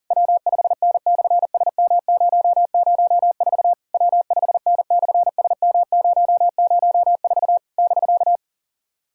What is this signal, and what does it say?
Morse code, heard as a keyed tone, about 37 wpm, 690 Hz. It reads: W5N/SM004 W5N/SM004 <BK>
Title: SOTA QSO Practice 40wpm